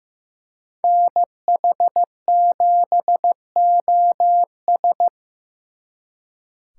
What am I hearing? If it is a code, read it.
Morse code, heard as a keyed tone, about 18 wpm, 700 Hz. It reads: NH7OS
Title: Speed Racing - USA 2x2 Call Signs 15wpm